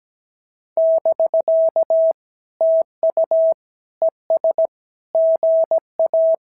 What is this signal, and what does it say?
Morse code, heard as a keyed tone, about 17 wpm, 650 Hz. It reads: <BK> TU ES GA
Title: SOTA QSO Practice 17wpm